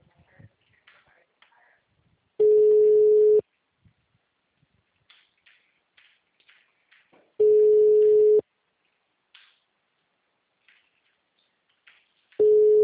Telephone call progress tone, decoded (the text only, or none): ringback tone